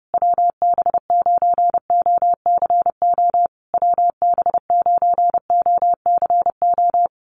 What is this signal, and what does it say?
Morse code, heard as a keyed tone, about 30 wpm, 700 Hz. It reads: WB9OCO WB9OCO
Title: USA Call Signs Repeated 3x 30wpm